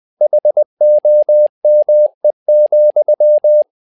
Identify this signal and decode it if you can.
Morse code, heard as a keyed tone, about 20 wpm, 600 Hz. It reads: HOME,
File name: learning center / tower trouble / tower trouble 20wpm